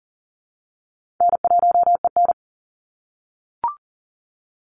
Morse code, heard as a keyed tone, about 40 wpm, 700 Hz. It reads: N1EN
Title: Easy Call Signs - 40wpm